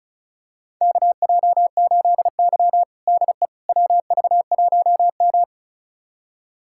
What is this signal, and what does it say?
Morse code, heard as a keyed tone, about 34 wpm, 700 Hz. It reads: KJ8Y DE WV1M
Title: USA Callsign DE Callsign - 4 Characters - 35wpm